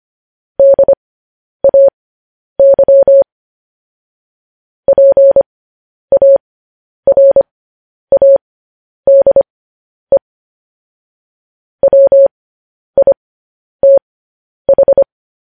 Morse code, heard as a keyed tone, about 25 wpm, spaced as for 10 wpm, 550 Hz. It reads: DAY PARADE WITH
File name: podcast / News headlines-1x-25x10